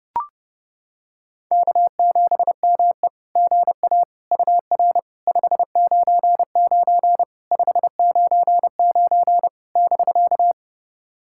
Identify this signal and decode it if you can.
Morse code, heard as a keyed tone, about 30 wpm, 700 Hz. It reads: K7ME GA UR 599 599 <BK>
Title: SOTA QSO Practice 30wpm